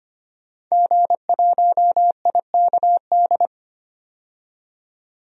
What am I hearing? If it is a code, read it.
Morse code, heard as a keyed tone, about 25 wpm, 700 Hz. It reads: G1IKD
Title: DXCC-100 Call Signs - 5 Characters - 25wpm